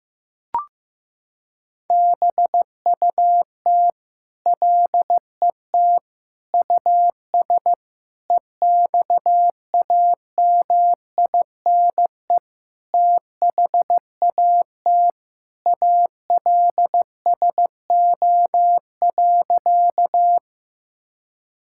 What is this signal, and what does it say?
Morse code, heard as a keyed tone, about 16 wpm, 700 Hz. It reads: BUT LET US EXAMINE THAT ALSO.
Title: Sentences from Top 1600 Words 15wpm